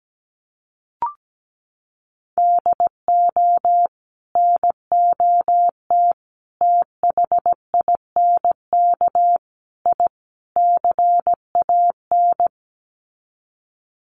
Morse code, heard as a keyed tone, about 17 wpm, 700 Hz. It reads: DO NOT THINK I CAN
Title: Sets of 5 Words Encore2 17wpm